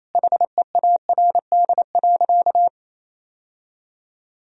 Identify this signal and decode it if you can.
Morse code, heard as a keyed tone, about 28 wpm, 700 Hz. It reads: HEARD.
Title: Sentences from Top 1000 Words - Review Part 1 - 28wpm